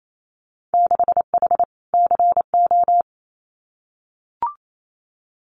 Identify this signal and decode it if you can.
Morse code, heard as a keyed tone, about 28 wpm, 690 Hz. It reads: 6H CO
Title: Winter Field Day Exchanges Encore 28wpm